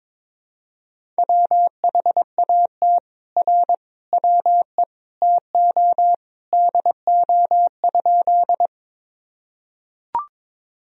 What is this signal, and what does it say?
Morse code, heard as a keyed tone, about 22 wpm, 700 Hz. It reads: WHAT R WE TO DO?
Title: Sentences from Top 100 Words and Abbreviations 22wpm